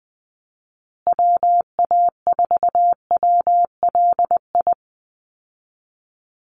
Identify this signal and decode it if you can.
Morse code, heard as a keyed tone, about 20 wpm, 700 Hz. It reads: WA4WLI
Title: Call Signs Repeated 2x - 20wpm